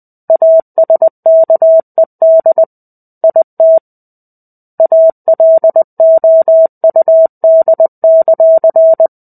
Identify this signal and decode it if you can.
Morse code, heard as a keyed tone, about 20 wpm, 650 Hz. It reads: ASKED IT ALOUD;
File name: 05-Chapter-05 20 WPM